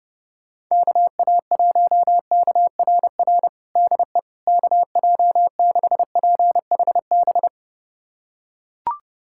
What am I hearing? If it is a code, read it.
Morse code, heard as a keyed tone, about 32 wpm, 700 Hz. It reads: KA1KRR DE KJ6PHB
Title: USA Callsign DE Callsign - 6 Characters - 30wpm